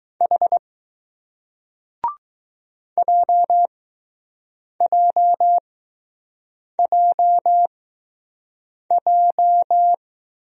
Morse code, heard as a keyed tone, about 20 wpm, spaced as for 6 wpm, 700 Hz. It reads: H JJJJ